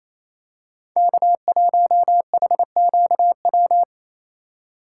Morse code, heard as a keyed tone, about 28 wpm, 700 Hz. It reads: K1HQW
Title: USA Call Signs - 5 Characters - 2x Character Spacing - 28wpm